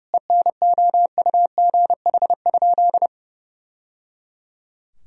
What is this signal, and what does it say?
Morse code, heard as a keyed tone, about 30 wpm, 700 Hz. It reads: ENOUGH?
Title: Sentences from Top 1000 Words - Review Part 2 - 30wpm